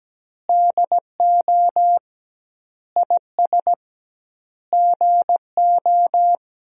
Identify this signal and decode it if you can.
Morse code, heard as a keyed tone, about 17 wpm, 700 Hz. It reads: DO IS GO
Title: 2x Word-Spacing - Sentences from Top 200 Words - 17wpm